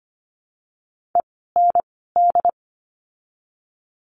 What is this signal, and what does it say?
Morse code, heard as a keyed tone, about 25 wpm, 700 Hz. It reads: E N D
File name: course / 25:10wpm / 094 Lesson 16f - Letter c - Words 25wpm